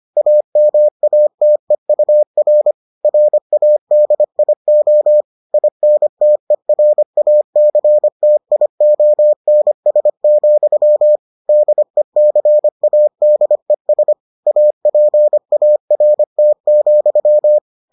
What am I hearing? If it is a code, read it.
Morse code, heard as a keyed tone, about 25 wpm, 600 Hz. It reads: AMATEUR RADIO INTERACTIONS, DECADES APART,